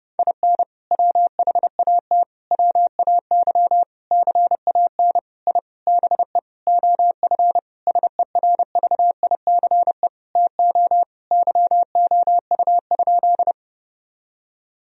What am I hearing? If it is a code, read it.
Morse code, heard as a keyed tone, about 30 wpm, 700 Hz. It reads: IN WHAT WAY CAN I BE OF SERVICE TO YOU?
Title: Sentences from Top 1000 Words - Review Part 4 - 30wpm